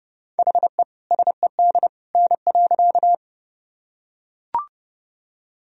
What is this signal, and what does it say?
Morse code, heard as a keyed tone, about 31 wpm, 700 Hz. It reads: HE SED N.
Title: Sentences from Top 100 Words and Abbreviations 30wpm